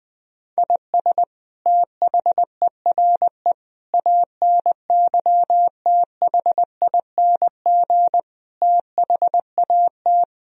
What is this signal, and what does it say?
Morse code, heard as a keyed tone, about 20 wpm, 700 Hz. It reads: IS THERE ANYTHING THAT
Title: Sentences from Top 1000 Words - Review Part 1 - 20wpm